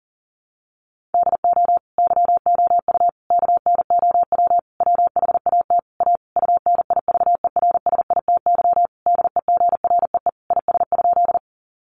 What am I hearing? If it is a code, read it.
Morse code, heard as a keyed tone, about 40 wpm, 700 Hz. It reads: DO YOU KNOW WHAT A UNIVERSITY DEGREE IS?
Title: Sentences from Top 1500 Words 40wpm